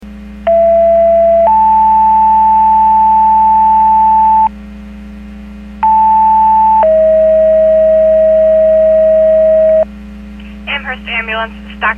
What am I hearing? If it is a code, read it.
Two-tone page, heard as a one-second tone, then a three-second tone, about 670 then 880 Hz, about 880 then 640 Hz, with speech. It is heard from 0.5 to 9.8 s.